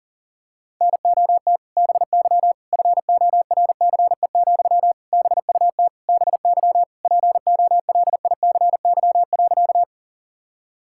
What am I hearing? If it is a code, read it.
Morse code, heard as a keyed tone, about 40 wpm, 700 Hz. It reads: NOT BY FORCE, BUT BY POLICY.
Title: Sentences from Top 1000 Words - Review Part 5 - 40wpm